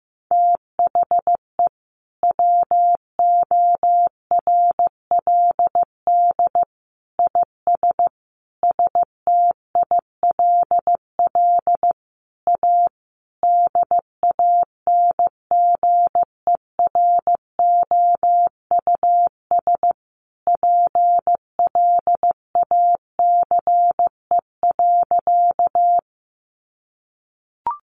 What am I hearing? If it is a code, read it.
Morse code, heard as a keyed tone, about 15 wpm, 700 Hz. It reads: THE WORLD IS STILL A DANGEROUS PLACE.
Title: Sentences from Top 1200 Words 15wpm